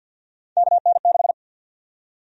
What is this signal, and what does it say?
Morse code, heard as a keyed tone, about 48 wpm, 700 Hz. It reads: KN6
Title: USA Call Sign Prefixes - 50wpm